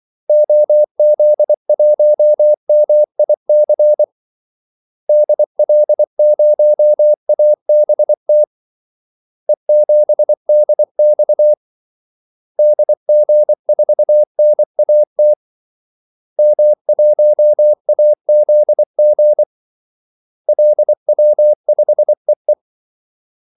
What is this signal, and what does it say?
Morse code, heard as a keyed tone, about 24 wpm, 600 Hz. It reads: OZ1MIC DL0ABT E7DX DG4NAT M1AZG LW5EE
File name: Callsigns-24wpm